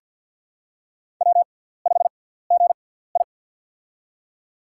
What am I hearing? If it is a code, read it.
Morse code, heard as a keyed tone, about 48 wpm, spaced as for 17 wpm, 700 Hz. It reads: W5GI
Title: USA Call Signs - 4 Characters - 6x Character Spacing - 50wpm